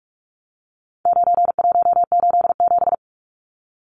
Morse code, heard as a keyed tone, about 44 wpm, 700 Hz. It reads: 9187
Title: Numbers - 4-Digit 45wpm